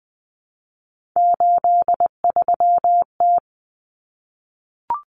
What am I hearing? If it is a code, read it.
Morse code, heard as a keyed tone, about 20 wpm, 700 Hz. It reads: 83T
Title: Common Cut Numbers - 3-Digit 20wpm